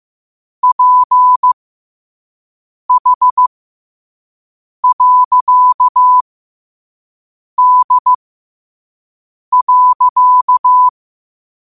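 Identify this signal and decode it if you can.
Morse code, heard as a keyed tone, about 15 wpm, spaced as for 5 wpm, 1000 Hz. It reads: PH.D.
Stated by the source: Morse code practice at 5 wpm.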